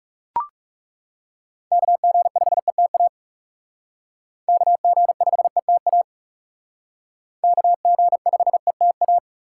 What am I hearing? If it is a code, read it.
Morse code, heard as a keyed tone, about 39 wpm, 700 Hz. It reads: KG5ETA KG5ETA KG5ETA